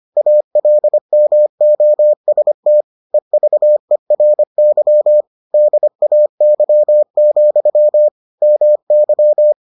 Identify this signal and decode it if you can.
Morse code, heard as a keyed tone, about 25 wpm, 600 Hz. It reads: ALMOST EVERY DAY, MY